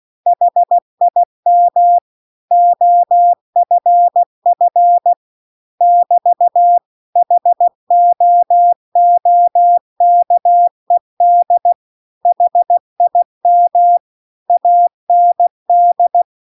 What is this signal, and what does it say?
Morse code, heard as a keyed tone, about 16 wpm, 700 Hz. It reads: HIM OFF <BT> HOOKED HIM AND